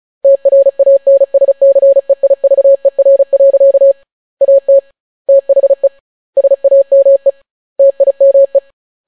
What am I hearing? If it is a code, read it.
Morse code, heard as a keyed tone, about 36 wpm, 550 Hz. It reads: TRANSCEIVER. AT THE SAME TIME